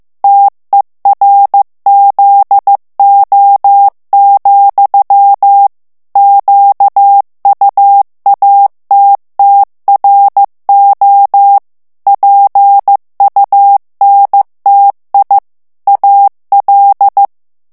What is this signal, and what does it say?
Morse code, heard as a keyed tone, about 15 wpm, 800 Hz. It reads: TERZO, QUATTRO PUNTI AL
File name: morse mp3 / ITALIEN texte clair 2 vitesse 14 mots minute